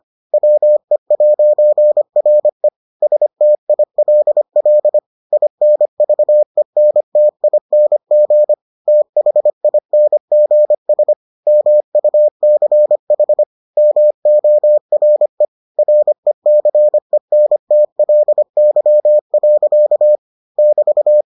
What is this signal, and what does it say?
Morse code, heard as a keyed tone, about 25 wpm, 600 Hz. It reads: WE'RE STILL INVENTING THINGS MUCH MORE RECENTLY. <BT>